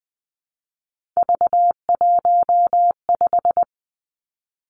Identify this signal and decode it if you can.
Morse code, heard as a keyed tone, about 20 wpm, 700 Hz. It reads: V15
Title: Three Challenging Letter-Number Combinations 20wpm